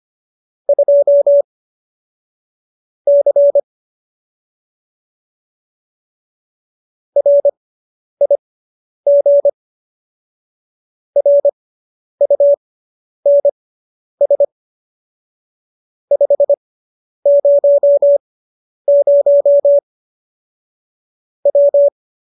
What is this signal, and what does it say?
Morse code, heard as a keyed tone, about 25 wpm, spaced as for 10 wpm, 570 Hz. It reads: 2 C RIG RUNS 500 W